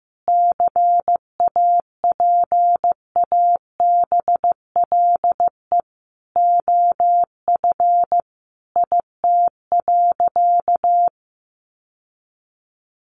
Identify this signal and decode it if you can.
Morse code, heard as a keyed tone, about 15 wpm, 700 Hz. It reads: CAPABLE OF IT.
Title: Sentences from Top 1200 Words 15wpm